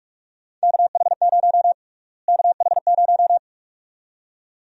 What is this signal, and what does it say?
Morse code, heard as a keyed tone, about 46 wpm, 690 Hz. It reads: KH0 KH0
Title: USA Call Sign Prefixes - Repeated 2x - 45wpm